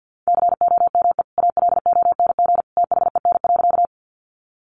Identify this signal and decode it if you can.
Morse code, heard as a keyed tone, about 50 wpm, 700 Hz. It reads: COME ALONG THEN.
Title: Sentences from Top 700 Words 50wpm